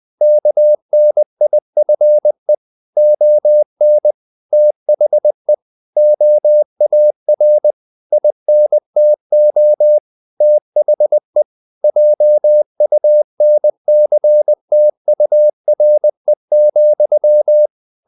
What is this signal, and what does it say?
Morse code, heard as a keyed tone, about 20 wpm, 600 Hz. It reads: KNIFE ON THE OAR INTO THE JUNCTURE,
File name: the old man and the sea part 2 20wpm